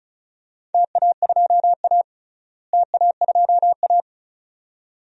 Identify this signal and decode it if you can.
Morse code, heard as a keyed tone, about 34 wpm, 700 Hz. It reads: TA2A TA2A
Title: DXCC-100 Call Signs - 4 Characters - Repeated 2x - 35wpm